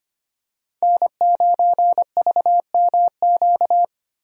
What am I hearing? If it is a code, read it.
Morse code, heard as a keyed tone, about 28 wpm, 700 Hz. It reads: N9VMQ
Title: Speed Racing - USA 1x3 Call Signs 25wpm